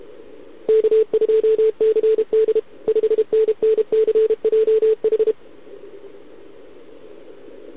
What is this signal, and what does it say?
Morse code, heard as a keyed tone, about 32 wpm, 440 Hz. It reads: K2CD 5NNCJH
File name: ARCHIV / OKDX2020 / OK1RIb